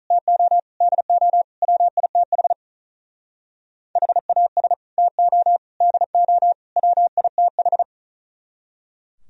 Morse code, heard as a keyed tone, about 43 wpm, 700 Hz. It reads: TO DO WITH HAS TO DO WITH
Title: Speed-Racing - Sets of 4 Words 35wpm